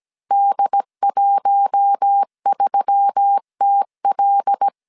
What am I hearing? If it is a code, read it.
Morse code, heard as a keyed tone, about 17 wpm, 800 Hz. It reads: D13TL
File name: CW / DET / 1 6 1500C S B 16wpm 2619